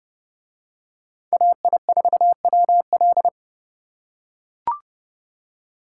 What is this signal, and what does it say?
Morse code, heard as a keyed tone, about 29 wpm, 700 Hz. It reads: AI4WL